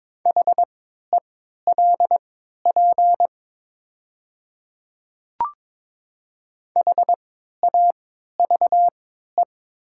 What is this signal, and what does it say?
Morse code, heard as a keyed tone, about 23 wpm, spaced as for 13 wpm, 700 Hz. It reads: HELP HAVE
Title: 3x Character Spacing - Top 200 Words - 22wpm